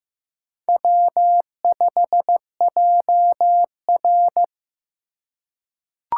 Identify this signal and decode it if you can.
Morse code, heard as a keyed tone, about 15 wpm, 700 Hz. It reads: W5JR
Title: USA Call Signs - 4 Characters - 3x Character Spacing - 15wpm